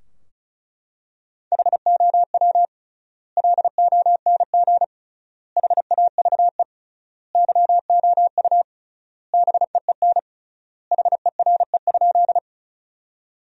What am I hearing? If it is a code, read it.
Morse code, heard as a keyed tone, about 35 wpm, 700 Hz. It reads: HOW LONG HAVE YOU BEEN HERE?
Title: 3x Word-Spacing - Sentences from Top 200 Words - 35wpm